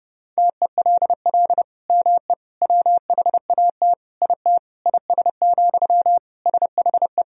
Morse code, heard as a keyed tone, about 32 wpm, 700 Hz. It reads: TELL ME WHAT IT IS, SHE
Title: Sentences from Top 1000 Words - Review Part 1 - 30wpm